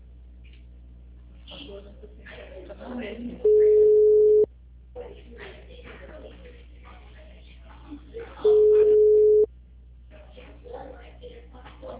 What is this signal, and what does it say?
Ringback tone (Europe, 425 Hz), with speech.
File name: out- -2012- - - .160